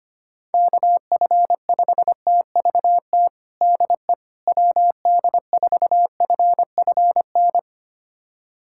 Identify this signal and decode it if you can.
Morse code, heard as a keyed tone, about 25 wpm, 700 Hz. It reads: KF5TVT DE WD4FFN
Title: USA Callsign DE Callsign - 6 Characters - 25wpm